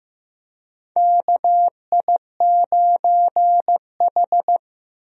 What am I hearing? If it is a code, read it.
Morse code, heard as a keyed tone, about 15 wpm, 700 Hz. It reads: KI9H
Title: USA Call Signs - 4 Characters - Repeated 2x - 15wpm